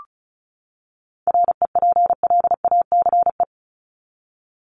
Morse code, heard as a keyed tone, about 34 wpm, 700 Hz. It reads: REPLACE
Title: Top 1251-1500 Words 35wpm